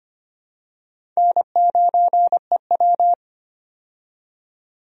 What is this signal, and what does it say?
Morse code, heard as a keyed tone, about 25 wpm, 700 Hz. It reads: N9EW
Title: USA Call Signs - 4 Characters - 25wpm